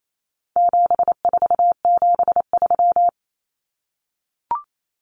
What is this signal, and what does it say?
Morse code, heard as a keyed tone, about 28 wpm, 700 Hz. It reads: 7473
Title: Common Cut Numbers - 4-Digit 28wpm